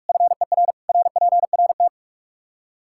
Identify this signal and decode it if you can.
Morse code, heard as a keyed tone, about 46 wpm, 700 Hz. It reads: FER RPRT
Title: Practice QSO - Exchange 2 - Info - 45wpm